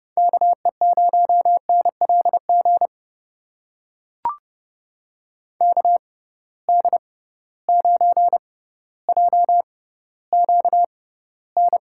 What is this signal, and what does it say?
Morse code, heard as a keyed tone, about 30 wpm, spaced as for 15 wpm, 700 Hz. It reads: KE0NLG KD9JQN
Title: USA Call Signs - 6 Characters - 6x Character Spacing - 30wpm